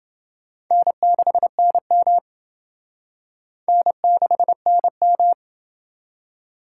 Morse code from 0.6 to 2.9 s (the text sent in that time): N6NM